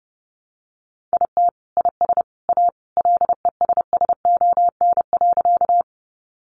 Morse code, heard as a keyed tone, about 30 wpm, 700 Hz. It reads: IT IS A LESSON.
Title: Sentences from Top 1200 Words 30wpm